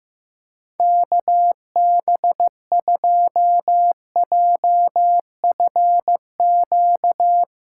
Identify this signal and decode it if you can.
Morse code, heard as a keyed tone, about 15 wpm, 700 Hz. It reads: KB2JFQ
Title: USA Callsign DE Callsign - 6 Characters - 15wpm